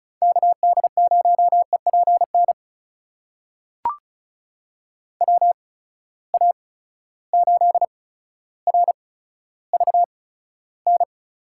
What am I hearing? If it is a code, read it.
Morse code, heard as a keyed tone, about 34 wpm, spaced as for 15 wpm, 700 Hz. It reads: KD0EPN WA8RVN
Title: USA Call Signs - 6 Characters - 8x Character Spacing - 35wpm